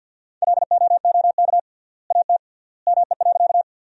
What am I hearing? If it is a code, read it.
Morse code, heard as a keyed tone, about 50 wpm, 700 Hz. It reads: LOOK AT ME.
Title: 3x Word-Spacing - Sentences from Top 200 Words - 50wpm